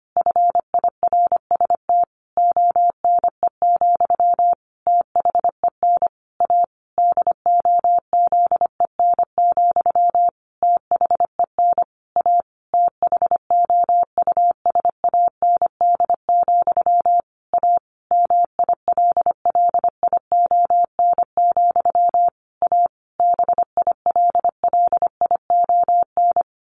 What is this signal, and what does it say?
Morse code, heard as a keyed tone, about 25 wpm, 700 Hz. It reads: FIRST ONE, THEN A DOZEN, THEN A THOUSAND, A MILLION, A BILLION
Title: A Sound Of Thunder 25wpm